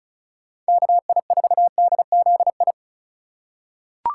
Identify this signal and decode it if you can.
Morse code, heard as a keyed tone, about 36 wpm, 700 Hz. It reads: KI4DZI
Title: USA Call Signs - 6 Characters - 6x Character Spacing - 35wpm